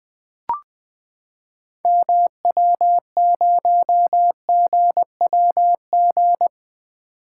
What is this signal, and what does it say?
Morse code, heard as a keyed tone, about 20 wpm, 700 Hz. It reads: MW0GWG
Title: 2023 POTA Activator Call Signs - 20wpm